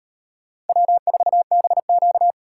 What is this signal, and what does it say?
Morse code, heard as a keyed tone, about 38 wpm, 700 Hz. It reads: W4BQ